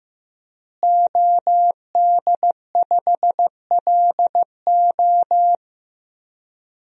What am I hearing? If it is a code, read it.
Morse code, heard as a keyed tone, about 15 wpm, 700 Hz. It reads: OD5LO
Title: DXCC-100 Call Signs - 5 Characters - Repeated 3x - 15wpm